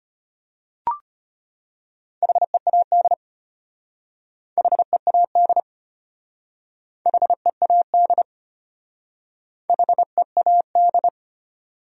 Morse code, heard as a keyed tone, about 33 wpm, 700 Hz. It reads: HEAD HEAD HEAD HEAD